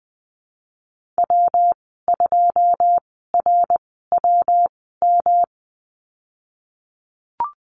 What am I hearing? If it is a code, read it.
Morse code, heard as a keyed tone, about 20 wpm, spaced as for 15 wpm, 700 Hz. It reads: W2RWM
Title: 173 Lesson 28g - Number 9 - Call Signs 20wpm